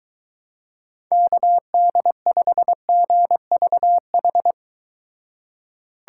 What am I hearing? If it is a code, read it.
Morse code, heard as a keyed tone, about 22 wpm, 700 Hz. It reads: KD5GVH